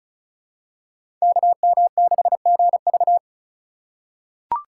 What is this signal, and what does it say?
Morse code, heard as a keyed tone, about 36 wpm, 700 Hz. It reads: KM6GV
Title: USA Call Signs - 5 Characters - 2x Character Spacing - 35wpm